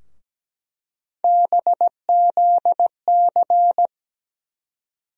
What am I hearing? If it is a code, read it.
Morse code, heard as a keyed tone, about 17 wpm, 700 Hz. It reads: BZC
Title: USA Call Sign Postfixes - 3 Letters 17wpm